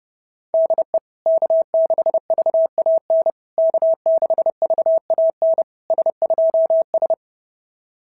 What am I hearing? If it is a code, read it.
Morse code, heard as a keyed tone, about 30 wpm, 650 Hz. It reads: DE K6VAN K6VAN S2S
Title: SOTA QSO Practice 30wpm